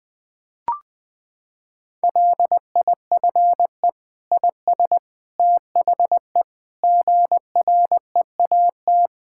Morse code, heard as a keyed tone, about 21 wpm, 700 Hz. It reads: LIFE IS THE GREAT
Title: Sentences from Top 1000 Words - Review Part 6 - 20wpm